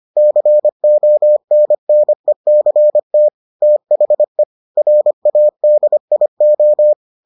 Morse code, heard as a keyed tone, about 25 wpm, 600 Hz. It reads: CONNECT THE RADIO